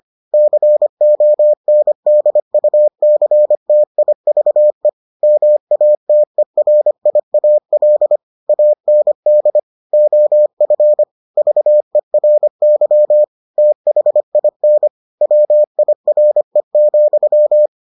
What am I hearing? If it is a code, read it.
Morse code, heard as a keyed tone, about 25 wpm, 600 Hz. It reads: CONDUCTIVE MATERIAL AND OF VERY THIN WIRE,